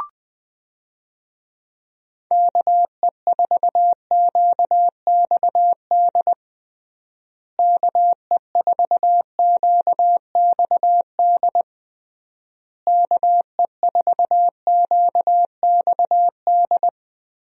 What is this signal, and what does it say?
Morse code, heard as a keyed tone, about 20 wpm, 700 Hz. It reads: KE4QXD KE4QXD KE4QXD
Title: USA Call Signs - 6 Characters - Repeated 3x - 20wpm